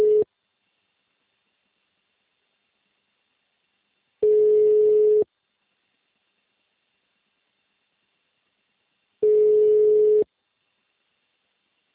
Ringback tone (Europe, 425 Hz).
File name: out- -2011- - - .36247